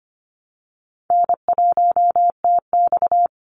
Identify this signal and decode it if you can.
Morse code, heard as a keyed tone, about 25 wpm, 700 Hz. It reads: N1TX
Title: Easy Call Signs - 25wpm